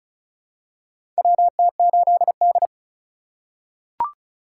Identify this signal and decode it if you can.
Morse code, heard as a keyed tone, about 36 wpm, 700 Hz. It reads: WT8D
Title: USA Call Signs - 4x Character Spacing - 35wpm